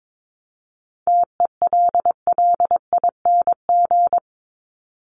Morse code, heard as a keyed tone, about 22 wpm, 700 Hz. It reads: TELLING
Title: ING Suffix - Top 100 Words with Base Words 22wpm